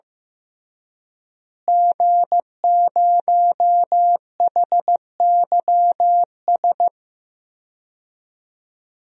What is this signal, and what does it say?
Morse code, heard as a keyed tone, about 15 wpm, 700 Hz. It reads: G0HYS
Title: DXCC-100 Call Signs - 5 Characters - Repeated 3x - 15wpm